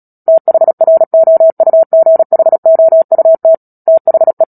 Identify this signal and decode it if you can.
Morse code, heard as a keyed tone, about 37 wpm, 650 Hz. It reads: THROUGHOUT THE
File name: Book Two / 10-Chapter 10 - Epilogue - 36 WPM